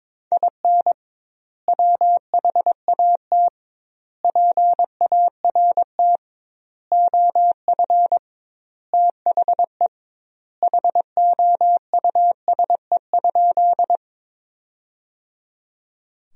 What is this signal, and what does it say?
Morse code, heard as a keyed tone, about 22 wpm, 700 Hz. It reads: IN WHAT PART OF THE HOUSE?
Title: 2x Word-Spacing - Sentences from Top 200 Words - 22wpm